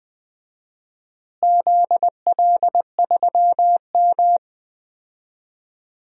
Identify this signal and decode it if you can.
Morse code, heard as a keyed tone, about 20 wpm, 700 Hz. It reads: ZL3M
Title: DXCC-100 Call Signs - 4 Characters - Repeated 2x - 20wpm